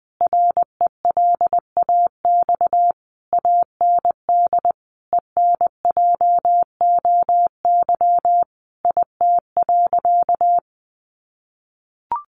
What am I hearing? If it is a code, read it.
Morse code, heard as a keyed tone, about 20 wpm, 700 Hz. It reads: RELAX AND ENJOY IT.
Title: Sentences from Top 1400 Words 20wpm